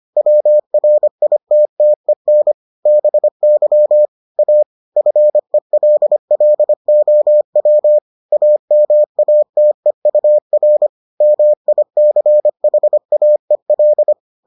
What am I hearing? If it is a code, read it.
Morse code, heard as a keyed tone, about 25 wpm, 600 Hz. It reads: WRITTEN BY A FELLOW AMATEUR MICHAEL